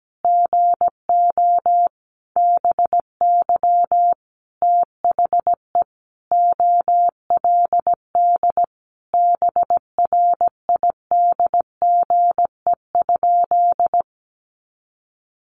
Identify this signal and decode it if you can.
Morse code, heard as a keyed tone, about 17 wpm, 700 Hz. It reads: GO BY THE OLD BRIDGE?
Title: Sentences from Top 1200 Words 17wpm